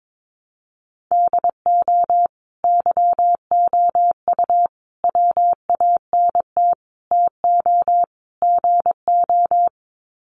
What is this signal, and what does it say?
Morse code, heard as a keyed tone, about 22 wpm, 700 Hz. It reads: DO YOU WANT TO GO
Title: Sets of 5 Words Encore2 22wpm